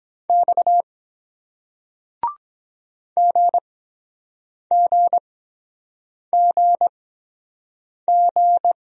Morse code, heard as a keyed tone, about 23 wpm, spaced as for 7 wpm, 700 Hz. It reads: X GGGG